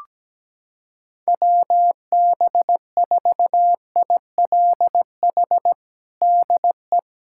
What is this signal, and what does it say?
Morse code, heard as a keyed tone, about 17 wpm, 700 Hz. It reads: WB4ILH DE
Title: USA Callsign DE Callsign 17wpm